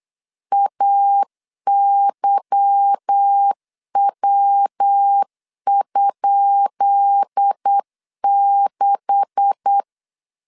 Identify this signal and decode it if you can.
Morse code, heard as a keyed tone, about 8 wpm, 800 Hz. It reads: AYW?6
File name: LEZIONI / 10 SIGNS / 10C DOT 8wpm 3649